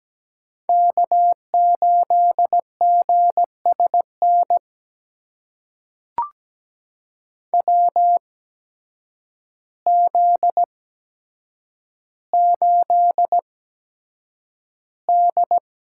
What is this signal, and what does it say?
Morse code, heard as a keyed tone, about 17 wpm, spaced as for 8 wpm, 700 Hz. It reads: K8GSN WZ8D
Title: USA Call Signs - 8x Character Spacing - 17wpm